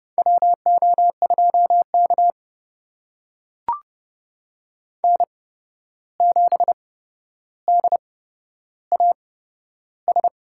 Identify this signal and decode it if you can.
Morse code, heard as a keyed tone, about 31 wpm, spaced as for 12 wpm, 700 Hz. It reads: WO2K N7DAS
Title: USA Call Signs - 8x Character Spacing - 30wpm